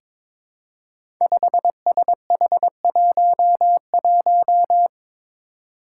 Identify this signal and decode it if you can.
Morse code, heard as a keyed tone, about 22 wpm, 700 Hz. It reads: 5SH11
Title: Five Challenging Letter-Number Combinations 22wpm